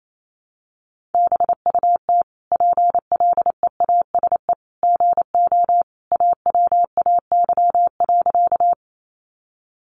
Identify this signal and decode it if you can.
Morse code, heard as a keyed tone, about 28 wpm, 700 Hz. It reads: BUT PLEASE GO AWAY.
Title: Sentences from Top 1500 Words 28wpm